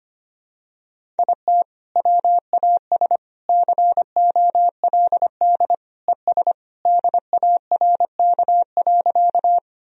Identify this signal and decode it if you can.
Morse code, heard as a keyed tone, about 25 wpm, 700 Hz. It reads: IT WAS COLD ES DARK.
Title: Sentences from Top 500 Words and Abbreviations 25wpm